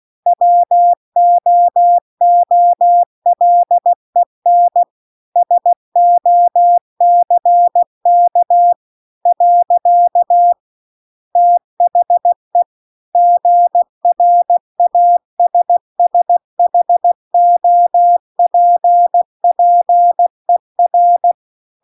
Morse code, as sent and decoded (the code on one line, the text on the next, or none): .-- --- --- .-.. -. / ... --- -.-. -.- / .-.-.- / - .... / --. .-. .- ... ... .... --- .--. .--. .-.
WOOLEN SOCK . THE GRASSHOPPER